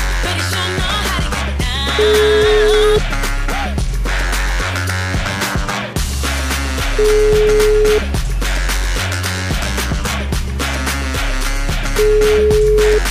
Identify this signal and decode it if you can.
Ringback tone (Europe, 425 Hz), with noise.